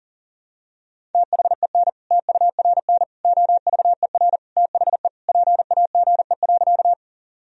Morse code, heard as a keyed tone, about 40 wpm, 700 Hz. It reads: THEN TURN OVER THE PAGE.
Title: Sentences from Top 1000 Words - Review Part 1 - 40wpm